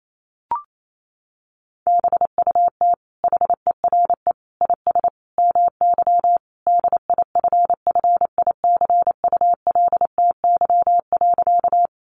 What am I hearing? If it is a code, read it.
Morse code, heard as a keyed tone, about 28 wpm, 700 Hz. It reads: BUT HERE IS MY DIFFICULTY.
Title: Sentences from Top 1500 Words 28wpm